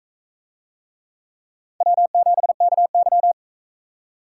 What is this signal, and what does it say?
Morse code, heard as a keyed tone, about 39 wpm, 700 Hz. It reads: W7KY